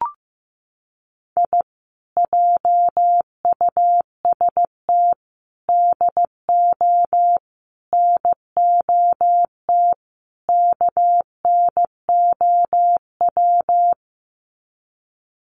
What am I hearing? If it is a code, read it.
Morse code, heard as a keyed tone, about 14 wpm, 700 Hz. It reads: I JUST DO NOT KNOW
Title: Sets of 5 Words Encore 15wpm